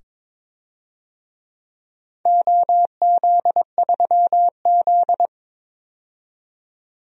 Morse code, heard as a keyed tone, about 22 wpm, 700 Hz. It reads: OZ3Z
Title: DXCC-100 Call Signs - 4 Characters - 22wpm